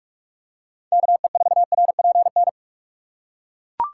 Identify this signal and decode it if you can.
Morse code, heard as a keyed tone, about 41 wpm, 700 Hz. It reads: KE4RPN
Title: USA Call Signs - 6 Characters - 2x Character Spacing - 45wpm